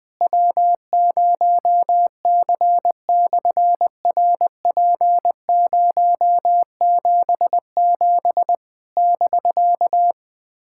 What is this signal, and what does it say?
Morse code, heard as a keyed tone, about 20 wpm, 700 Hz. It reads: W0C/RP077 <BK>
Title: SOTA QSO Practice 20wpm